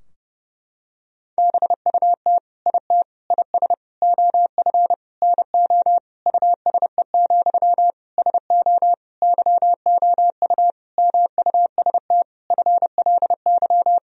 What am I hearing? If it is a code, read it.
Morse code, heard as a keyed tone, about 30 wpm, 700 Hz. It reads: BUT IT IS OF NO USE, SO YOU MUST FLY
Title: Sentences from Top 1000 Words - Review Part 2 - 30wpm